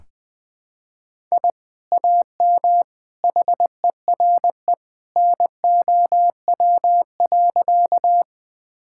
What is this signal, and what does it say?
Morse code, heard as a keyed tone, about 20 wpm, 700 Hz. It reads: I AM HERE NOW.
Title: Sentences with 2-Word Phrases - 20wpm